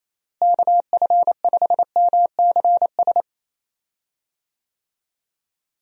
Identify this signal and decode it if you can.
Morse code, heard as a keyed tone, about 28 wpm, 700 Hz. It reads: KF5MCS
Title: USA Call Signs - 6 Characters - Repeated 2x - 28wpm